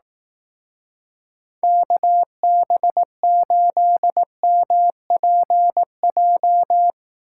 Morse code, heard as a keyed tone, about 18 wpm, 700 Hz. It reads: KB8MPJ
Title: Speed Racing - USA 2x3 Call Signs 15wpm